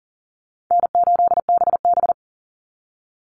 Morse code, heard as a keyed tone, about 40 wpm, 700 Hz. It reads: N8BB
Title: NA SOTA Chaser Callsigns 40wpm